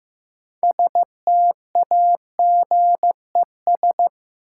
Morse code, heard as a keyed tone, about 15 wpm, 700 Hz. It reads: STAGES
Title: Sets of 5 Words Encore - New Word Drill - 15wpm